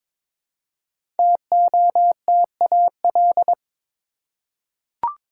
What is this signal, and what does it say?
Morse code, heard as a keyed tone, about 22 wpm, 700 Hz. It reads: TOTAL
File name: 5 Letter Words - First 3 Letters Given - 22wpm